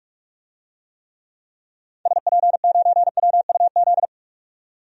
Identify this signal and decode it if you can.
Morse code, heard as a keyed tone, about 47 wpm, 700 Hz. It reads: SP9WUZ
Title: DXCC-100 Call Signs - 6 Characters - 45wpm